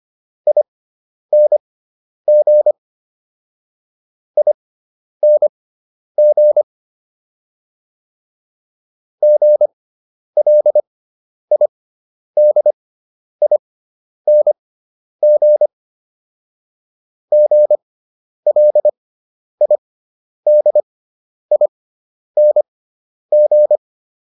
Morse code, heard as a keyed tone, about 25 wpm, spaced as for 10 wpm, 600 Hz. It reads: ING ING GLIDING GLIDING